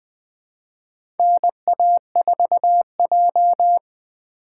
Morse code, heard as a keyed tone, about 20 wpm, 700 Hz. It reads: NA4J
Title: USA Call Signs - 4 Characters - 4x Character Spacing - 20wpm